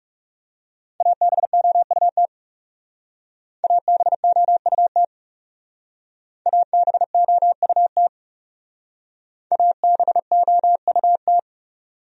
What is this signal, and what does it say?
Morse code, heard as a keyed tone, about 36 wpm, 700 Hz. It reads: ABOUT ABOUT ABOUT ABOUT
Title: Speed-Racing Top 100 Words 30wpm